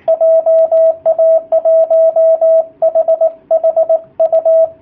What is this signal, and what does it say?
Morse code, heard as a keyed tone, about 18 wpm, 660 Hz, with radio noise. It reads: JA1HHU
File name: jh8cbh / 2019cw / sp18- w